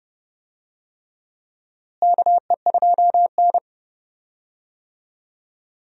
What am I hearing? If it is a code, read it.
Morse code, heard as a keyed tone, about 30 wpm, 700 Hz. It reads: KE2N
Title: USA Call Signs - 4 Characters - Repeated 3x - 30wpm